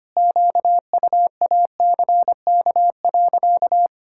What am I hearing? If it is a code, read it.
Morse code, heard as a keyed tone, about 25 wpm, 700 Hz. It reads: QUACK.
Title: Pangram Challenge v2 - 25wpm